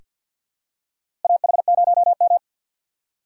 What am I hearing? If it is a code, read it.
Morse code, heard as a keyed tone, about 47 wpm, 700 Hz. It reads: AH0M
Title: USA Call Signs - 4 Characters - 2x Character Spacing - 50wpm